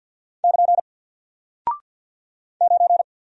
Morse code, heard as a keyed tone, about 48 wpm, 700 Hz. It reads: <KN> 9
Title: Single Character (Letter:Number:Punctuation:Prosign) Alt 50wpm